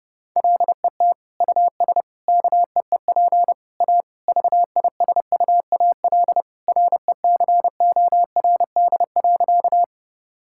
Morse code, heard as a keyed tone, about 30 wpm, 700 Hz. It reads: LET US KEEP A VISUAL RECORD.
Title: Sentences from Top 1000 Words - Review Part 6 - 30wpm